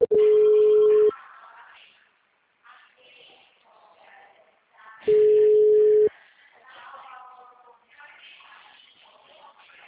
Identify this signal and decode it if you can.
Ringback tone (Europe, 425 Hz), with speech.